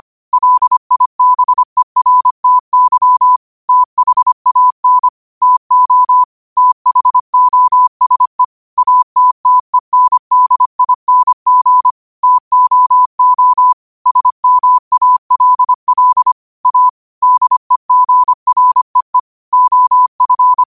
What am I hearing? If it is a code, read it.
Morse code, heard as a keyed tone, about 25 wpm, 1000 Hz. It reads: LIBERTY THAN TO THOSE ATTENDING TOO SMALL A DEGREE OF
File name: qotdfiles / 25wpm